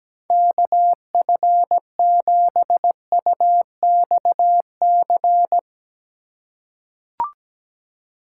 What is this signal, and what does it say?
Morse code, heard as a keyed tone, about 17 wpm, 700 Hz. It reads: KF7UXC
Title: K1USN SST QSO Practice - 17wpm